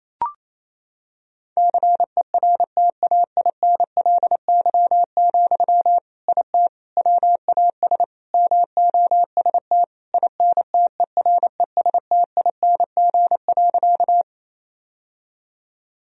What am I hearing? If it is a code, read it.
Morse code, heard as a keyed tone, about 28 wpm, 700 Hz. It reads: CERTAINLY, IT WAS MOST INTERESTING.
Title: Sentences from Top 1000 Words - Review Part 1 - 28wpm